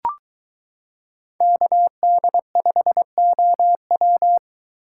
Morse code, heard as a keyed tone, about 23 wpm, 700 Hz. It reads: KD5OW